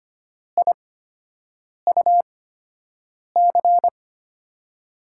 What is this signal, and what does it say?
Morse code, heard as a keyed tone, about 25 wpm, spaced as for 7 wpm, 700 Hz. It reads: IUC
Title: USA Call Signs - 6 Characters - 8x Character Spacing - 25wpm